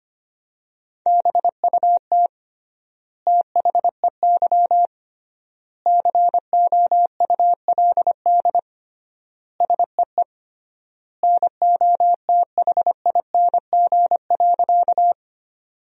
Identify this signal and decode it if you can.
Morse code, heard as a keyed tone, about 25 wpm, 700 Hz. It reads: BUT THEY COULD SEE NOTHING.
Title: 3x Word-Spacing - Sentences from Top 500 Words - 25wpm